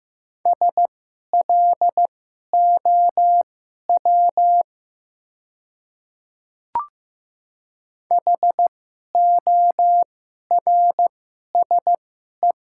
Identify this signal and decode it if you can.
Morse code, heard as a keyed tone, about 15 wpm, spaced as for 11 wpm, 700 Hz. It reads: SLOW HORSE
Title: 2x Character Spacing - Top 500 Words - 15wpm